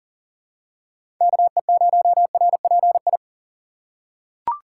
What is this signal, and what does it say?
Morse code, heard as a keyed tone, about 40 wpm, 700 Hz. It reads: KE0RPI
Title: USA Call Signs - 6 Characters - 4x Character Spacing - 40wpm